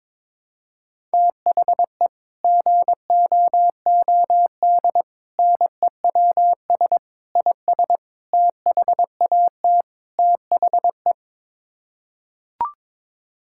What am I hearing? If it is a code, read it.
Morse code, heard as a keyed tone, about 22 wpm, 700 Hz. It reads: THE GOOD NEWS IS THAT THE
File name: practice / 22wpm / Sets of 6 Words - 22wpm